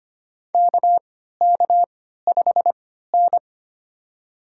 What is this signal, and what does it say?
Morse code, heard as a keyed tone, about 25 wpm, spaced as for 14 wpm, 700 Hz. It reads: KK5N
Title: USA Call Signs - 3x Character Spacing - 25wpm